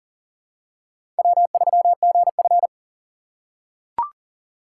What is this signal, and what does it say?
Morse code, heard as a keyed tone, about 40 wpm, 700 Hz. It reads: W3GF
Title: USA Call Signs - 4 Characters - 4x Character Spacing - 40wpm